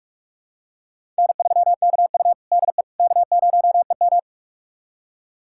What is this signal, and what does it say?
Morse code, heard as a keyed tone, about 46 wpm, 700 Hz. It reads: N3KU DE K0EM
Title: USA Callsign DE Callsign - 4 Characters - 45wpm